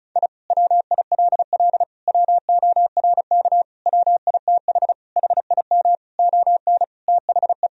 Morse code, heard as a keyed tone, about 36 wpm, 700 Hz. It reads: I WILL WORK WITH HIM ON THE
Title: Sentences from Top 1000 Words - Review Part 4 - 35wpm